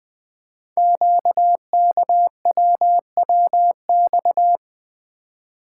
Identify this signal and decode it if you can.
Morse code, heard as a keyed tone, about 20 wpm, 700 Hz. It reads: QKWWX
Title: Code Groups - 5 Letters - 20wpm